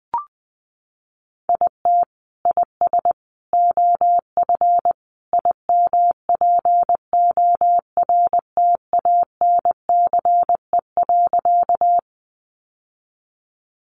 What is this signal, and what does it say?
Morse code, heard as a keyed tone, about 20 wpm, 700 Hz. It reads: IT IS OF IMPORTANCE.
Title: Sentences from Top 1200 Words 20wpm